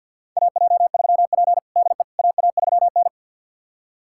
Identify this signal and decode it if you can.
Morse code, heard as a keyed tone, about 49 wpm, 700 Hz. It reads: AJ3P DE AA3N
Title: USA Callsign DE Callsign - 4 Characters - 50wpm